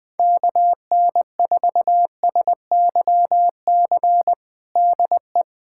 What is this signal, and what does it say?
Morse code, heard as a keyed tone, about 20 wpm, 700 Hz. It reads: KN4SYC DE
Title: USA Callsign DE Callsign - 6 Characters - 20wpm